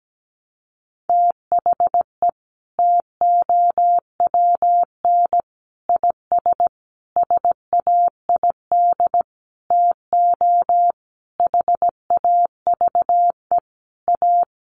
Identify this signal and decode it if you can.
Morse code, heard as a keyed tone, about 17 wpm, 700 Hz. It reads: THE TOWN IS SAID TO HAVE A
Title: Sentences from Top 1300 Words 17wpm